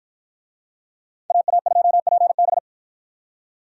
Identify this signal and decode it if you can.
Morse code, heard as a keyed tone, about 53 wpm, 700 Hz. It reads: AA2PB